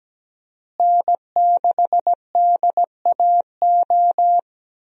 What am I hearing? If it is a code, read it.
Morse code, heard as a keyed tone, about 17 wpm, 700 Hz. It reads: N6DAO
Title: USA Call Signs - 5 Characters - Repeated 2x - 17wpm